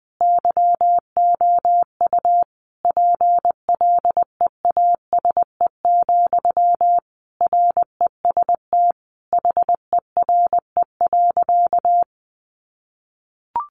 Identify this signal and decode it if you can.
Morse code, heard as a keyed tone, about 20 wpm, 700 Hz. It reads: YOU PLEASE, REST HERE.
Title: Sentences from Top 1500 Words 20wpm